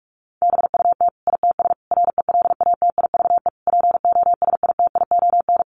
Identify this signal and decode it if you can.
Morse code, heard as a keyed tone, about 44 wpm, 700 Hz. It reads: BUT ITS RELATIVE POSITION
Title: Sentences from Top 1400 Words 45wpm